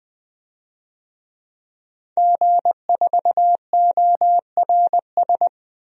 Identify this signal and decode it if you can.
Morse code, heard as a keyed tone, about 20 wpm, 700 Hz. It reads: G4ORS
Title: DXCC-100 Call Signs - 5 Characters - 20wpm